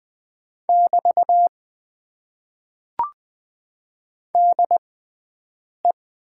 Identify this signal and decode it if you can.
Morse code, heard as a keyed tone, about 20 wpm, spaced as for 7 wpm, 700 Hz. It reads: <BT> DE
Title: Top 90 QSO Elements - 6x Character Spacing - 20wpm